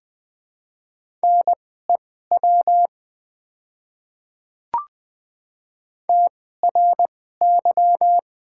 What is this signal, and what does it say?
Morse code, heard as a keyed tone, about 20 wpm, spaced as for 14 wpm, 700 Hz. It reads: NEW TRY
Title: 2x Character Spacing - Top 200 Words - 20wpm